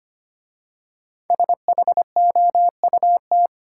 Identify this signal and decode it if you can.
Morse code, heard as a keyed tone, about 25 wpm, 700 Hz. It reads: SHOUT
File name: 5 Letter Words - First 3 Letters Given - 25wpm